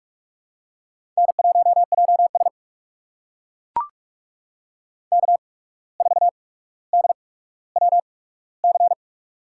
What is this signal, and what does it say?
Morse code, heard as a keyed tone, about 42 wpm, spaced as for 17 wpm, 700 Hz. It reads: N1JS K4DWC
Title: USA Call Signs - 8x Character Spacing - 45wpm